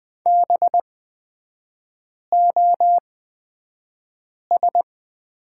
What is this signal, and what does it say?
Morse code, heard as a keyed tone, about 20 wpm, spaced as for 5 wpm, 700 Hz. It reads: BOS
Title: 224 Lesson 36e - BT - Three Characters 20wpm